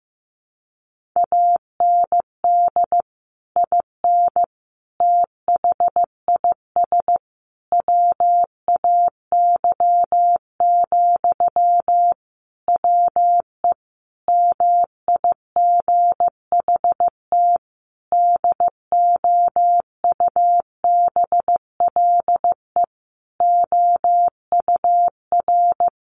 Morse code, heard as a keyed tone, about 15 wpm, 700 Hz. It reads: AND IN THIS WAY, WE MIGHT DOUBLE OUR
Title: Sentences from Top 1400 Words 15wpm